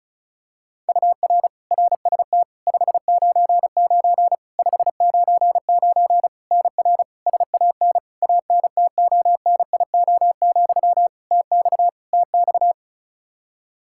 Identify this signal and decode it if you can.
Morse code, heard as a keyed tone, about 34 wpm, 700 Hz. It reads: UR RST 599 599 NR SAN ANTONIO, TX TX
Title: K1USN SST QSO Practice 35wpm